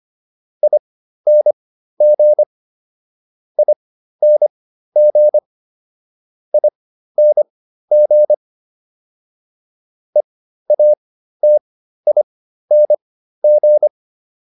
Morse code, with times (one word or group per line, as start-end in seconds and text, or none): 0.6-2.4 ING
3.6-5.4 ING
6.5-8.3 ING
10.2-13.9 EATING